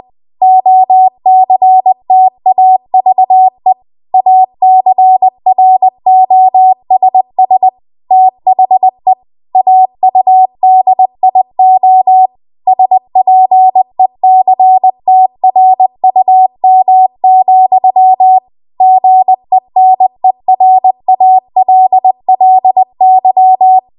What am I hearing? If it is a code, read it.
Morse code, heard as a keyed tone, about 20 wpm, 750 Hz. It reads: OCTAVE ACROSS THE AUDIO SPECTRUM, GENERALLY